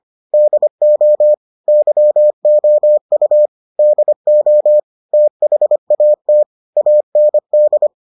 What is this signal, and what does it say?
Morse code, heard as a keyed tone, about 25 wpm, 600 Hz. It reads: DO YOU DO THAT AND